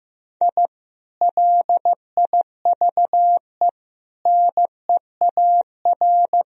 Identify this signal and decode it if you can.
Morse code, heard as a keyed tone, about 15 wpm, 700 Hz. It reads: I LIVE NEAR
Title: Sentences from Top 1000 Words - Review Part 3 - 15wpm